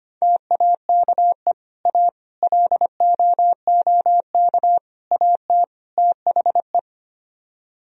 Morse code, heard as a keyed tone, about 32 wpm, 700 Hz. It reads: TAKE A LOOK AT THE